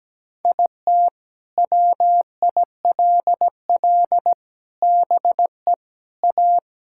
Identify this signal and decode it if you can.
Morse code, heard as a keyed tone, about 17 wpm, 700 Hz. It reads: IT WILL BE A
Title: Sentences from Top 1000 Words - Review Part 6 - 17wpm